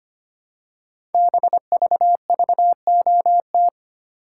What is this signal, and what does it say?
Morse code, heard as a keyed tone, about 25 wpm, 700 Hz. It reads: BVVOT
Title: Code Groups - 5 Letter:Numbers - 25wpm